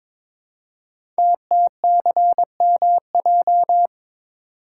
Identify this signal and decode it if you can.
Morse code, heard as a keyed tone, about 22 wpm, 700 Hz. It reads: TTCMJ
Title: Code Groups - 5 Letters - 22wpm